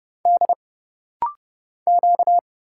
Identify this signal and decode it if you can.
Morse code, heard as a keyed tone, about 30 wpm, 700 Hz. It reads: D Q
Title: Single Letter-Number Alt - Rapid-Fire 30wpm